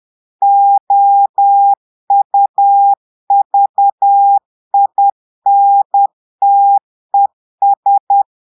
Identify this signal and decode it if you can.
Morse code, heard as a keyed tone, about 10 wpm, 800 Hz. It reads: OUVINTES
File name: portugais texte en clair 2 vitesse 10 mots minutes